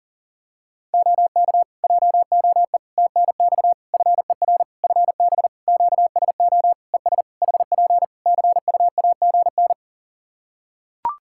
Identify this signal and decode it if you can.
Morse code, heard as a keyed tone, about 40 wpm, 700 Hz. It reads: OK JOE TNX FER FB QSO ES HP CUAGN
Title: K1USN SST QSO Practice 40wpm